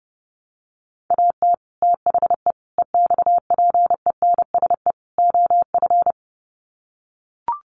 Morse code, heard as a keyed tone, about 30 wpm, 700 Hz. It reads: AT THE EXPENSE OF
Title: Sets of 4 Words 30wpm